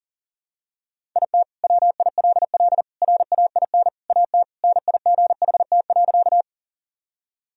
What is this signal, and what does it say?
Morse code, heard as a keyed tone, about 40 wpm, 700 Hz. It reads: IT WILL RAIN AT NIGHT.
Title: Sentences from Top 1000 Words - Review Part 1 - 40wpm